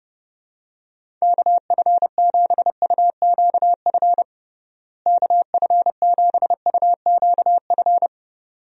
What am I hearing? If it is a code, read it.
Morse code, heard as a keyed tone, about 30 wpm, 700 Hz. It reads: KF7UQF KF7UQF
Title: USA Call Signs - 6 Characters - Repeated 2x - 30wpm